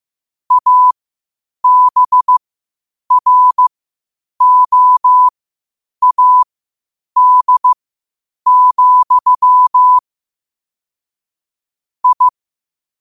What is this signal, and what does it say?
Morse code, heard as a keyed tone, about 15 wpm, spaced as for 8 wpm, 1000 Hz. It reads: ABROAD, I
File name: qotdfiles / 8wpm